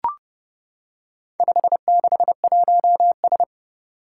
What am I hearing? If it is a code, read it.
Morse code, heard as a keyed tone, about 28 wpm, 700 Hz. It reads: 561S
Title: Four Challenging Letter-Number Combinations 30wpm